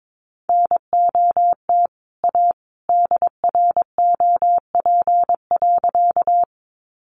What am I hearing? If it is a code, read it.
Morse code, heard as a keyed tone, about 22 wpm, 700 Hz. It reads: NOT A DROP.
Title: Sentences from Top 1400 Words 22wpm